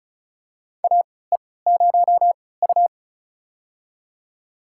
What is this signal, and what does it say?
Morse code, heard as a keyed tone, about 36 wpm, spaced as for 20 wpm, 700 Hz. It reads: AE0U
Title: USA Call Signs - 4 Characters - 3x Character Spacing - 35wpm